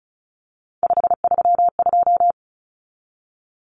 Morse code, heard as a keyed tone, about 34 wpm, 700 Hz. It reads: 532
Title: Numbers - 3-Digit 35wpm